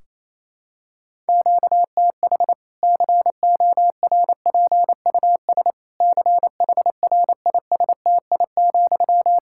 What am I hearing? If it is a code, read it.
Morse code, heard as a keyed tone, about 28 wpm, 700 Hz. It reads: QTH CORPUS CHRISTI,
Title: Practice QSO - Exchange 1 - Beginning - 28wpm